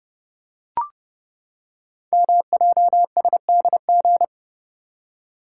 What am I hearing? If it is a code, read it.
Morse code, heard as a keyed tone, about 30 wpm, 700 Hz. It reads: MJSDG
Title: Code Groups - 5 Letters - 30wpm